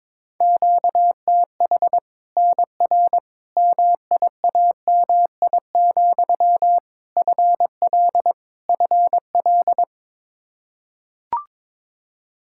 Morse code, heard as a keyed tone, about 21 wpm, 700 Hz. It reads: QTH NR MIAMI, FL FL
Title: Practice QSO - Exchange 1 - Beginning - 22wpm